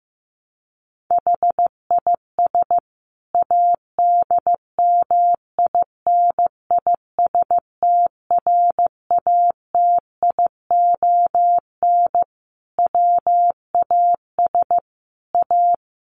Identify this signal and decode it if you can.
Morse code, heard as a keyed tone, about 15 wpm, 700 Hz. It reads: HIS ADMINISTRATION WAS A
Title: Sentences from Top 1300 Words 15wpm